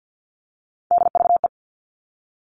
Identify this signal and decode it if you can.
Morse code, heard as a keyed tone, about 50 wpm, 700 Hz. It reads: DVE
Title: All Cut Numbers - 3-Digit 50wpm